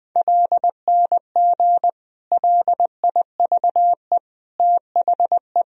Morse code, heard as a keyed tone, about 21 wpm, 700 Hz. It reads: LNG LIVE THE
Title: Sentences from Top 400 Words and Abbreviations 20wpm